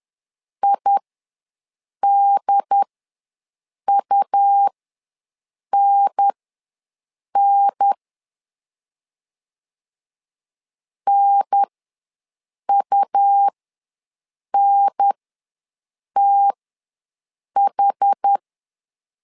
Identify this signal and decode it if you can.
Morse code, heard as a keyed tone, about 10 wpm, spaced as for 6 wpm, 800 Hz. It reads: IDUNN NUNTH
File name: LEZIONI / 3 MWRO / 3 M ws 10wpm 2121